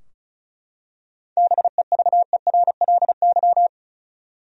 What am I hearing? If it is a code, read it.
Morse code, heard as a keyed tone, about 34 wpm, 700 Hz. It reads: BEVERLY
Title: USA Common Names - 35wpm